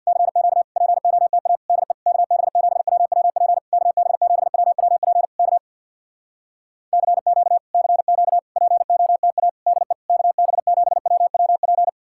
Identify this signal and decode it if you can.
Morse code, heard as a keyed tone, about 64 wpm, 700 Hz. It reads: CQ POTA DE KB7WWP KB7WWP K CQ CQ POTA DE KB7WWP